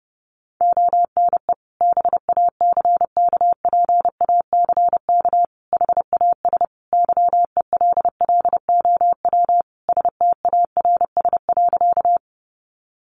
Morse code, heard as a keyed tone, about 30 wpm, 700 Hz. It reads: ONE BACKPACK HAS YELLOW STARS.
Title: Additional Easy Phrases 30wpm